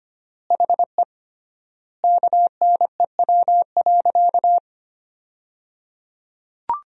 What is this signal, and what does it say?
Morse code, heard as a keyed tone, about 25 wpm, 700 Hz. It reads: HE KNEW.
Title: 3x Word-Spacing - Sentences from Top 400 Words - 25wpm